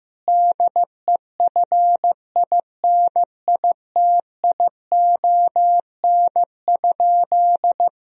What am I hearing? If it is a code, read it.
Morse code, heard as a keyed tone, about 15 wpm, 700 Hz. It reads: DEFINITION?
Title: Sentences from Top 1000 Words - Review Part 4 - 15wpm